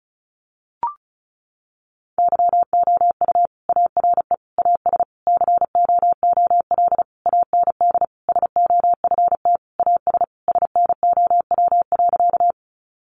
Morse code, heard as a keyed tone, about 34 wpm, 700 Hz. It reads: YOU ARE AS COOL AND SOFT AS SNOW.
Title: Sentences from Top 1500 Words 35wpm